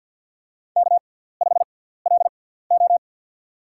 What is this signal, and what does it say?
Morse code, heard as a keyed tone, about 51 wpm, spaced as for 17 wpm, 700 Hz. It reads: K5LO
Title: USA Call Signs - 4 Characters - 6x Character Spacing - 50wpm